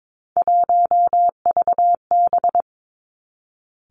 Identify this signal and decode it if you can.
Morse code, heard as a keyed tone, about 22 wpm, 700 Hz. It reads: 1VB
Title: Three Challenging Letter-Number Combinations 22wpm